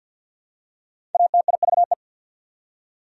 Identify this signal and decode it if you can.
Morse code, heard as a keyed tone, about 51 wpm, 700 Hz. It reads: ATIVE
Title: Prefixes and Suffixes 50wpm